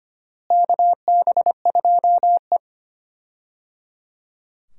Morse code, heard as a keyed tone, about 25 wpm, 700 Hz. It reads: KB2E
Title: USA Call Signs - 4 Characters - Repeated 3x - 25wpm